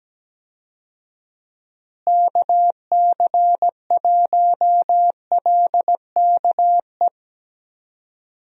Morse code, heard as a keyed tone, about 17 wpm, 700 Hz. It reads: KC1LKE
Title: USA Call Signs - 6 Characters - Repeated 3x - 17wpm